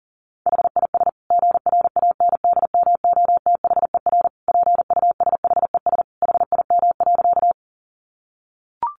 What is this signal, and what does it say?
Morse code, heard as a keyed tone, about 40 wpm, 700 Hz. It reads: HIS GRANDMOTHER PUSHES HIM.
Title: Additional Easy Phrases 40wpm